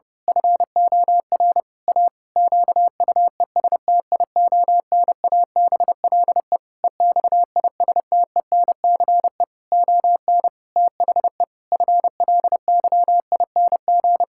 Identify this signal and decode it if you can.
Morse code, heard as a keyed tone, about 30 wpm, 700 Hz. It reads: FOR A QUESTIONABLE EXISTENCE ON THE FLYING
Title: Pangram Challenge v2 - 30wpm